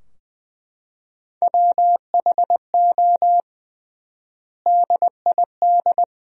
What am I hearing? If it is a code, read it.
Morse code, heard as a keyed tone, about 21 wpm, 700 Hz. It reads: WHO DID
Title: 3x Word-Spacing - Sentences from Top 300 Words - 20wpm